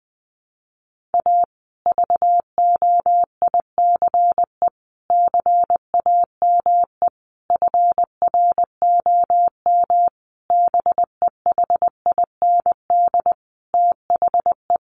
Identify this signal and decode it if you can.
Morse code, heard as a keyed tone, about 20 wpm, 700 Hz. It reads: A VOICE CAME FROM BEHIND THE
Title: Sentences from Top 1400 Words 20wpm